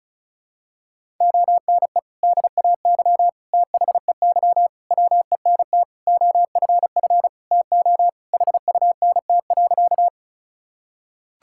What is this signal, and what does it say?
Morse code, heard as a keyed tone, about 34 wpm, 700 Hz. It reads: ONE DAY THEY WENT OFF TO HUNT.
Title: Sentences from Top 1600 Words 35wpm